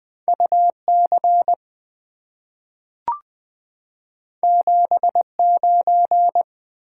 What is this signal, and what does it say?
Morse code, heard as a keyed tone, about 20 wpm, 700 Hz. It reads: UC 79
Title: Two Letter-Number Combinations - 20wpm